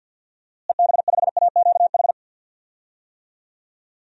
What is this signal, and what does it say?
Morse code, heard as a keyed tone, about 46 wpm, 700 Hz. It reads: EB5AQH
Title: DXCC-100 Call Signs - 6 Characters - Repeated 3x - 50wpm